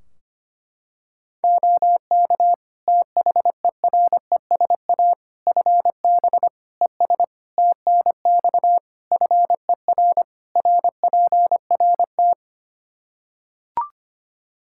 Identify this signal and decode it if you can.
Morse code, heard as a keyed tone, about 25 wpm, 700 Hz. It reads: OK THERESA FB ES TNX FER RPRT
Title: Practice QSO - Exchange 2 - Info - 25wpm